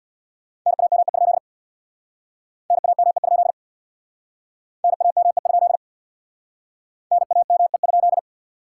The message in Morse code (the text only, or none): NAME? NAME? NAME? NAME?